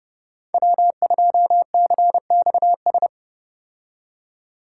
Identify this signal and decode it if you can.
Morse code, heard as a keyed tone, about 29 wpm, 700 Hz. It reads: W2CXS
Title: Code Groups - 5 Letter:Numbers - 30wpm